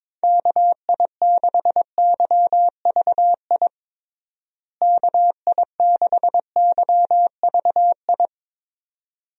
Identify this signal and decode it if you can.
Morse code, heard as a keyed tone, about 22 wpm, 700 Hz. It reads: KI6YVI KI6YVI
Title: USA Call Signs - 6 Characters - Repeated 2x - 22wpm